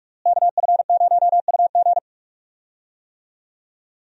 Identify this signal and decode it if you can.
Morse code, heard as a keyed tone, about 46 wpm, 700 Hz. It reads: KF0UG
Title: USA Call Signs - 5 Characters - Repeated 3x - 45wpm